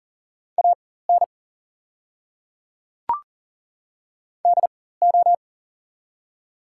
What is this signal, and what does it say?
Morse code, heard as a keyed tone, about 40 wpm, spaced as for 19 wpm, 700 Hz. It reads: AN DO
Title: 4x Character Spacing - Top 200 Words - 40wpm